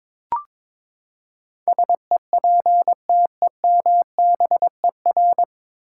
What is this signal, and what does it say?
Morse code, heard as a keyed tone, about 22 wpm, 700 Hz. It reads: SEPTEMBER
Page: Morse Code Ninja practice 22wpm